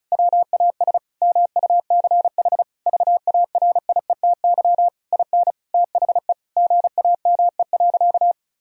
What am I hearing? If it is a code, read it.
Morse code, heard as a keyed tone, about 36 wpm, 700 Hz. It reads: WAS MUCH VARIETY IN THE GAME.
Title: Sentences from Top 1000 Words - Review Part 4 - 35wpm